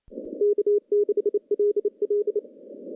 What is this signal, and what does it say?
Morse code, heard as a keyed tone, about 28 wpm, 410 Hz. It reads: K6LL
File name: contests / CW-SS-k5tr- -0600